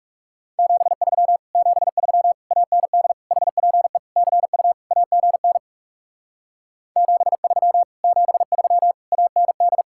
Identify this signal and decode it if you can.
Morse code, heard as a keyed tone, about 40 wpm, 700 Hz. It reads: 73 73 AND HPE CU AGN 73 73 AND